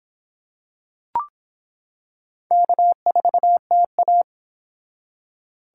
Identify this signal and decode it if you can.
Morse code, heard as a keyed tone, about 24 wpm, 700 Hz. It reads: K4TA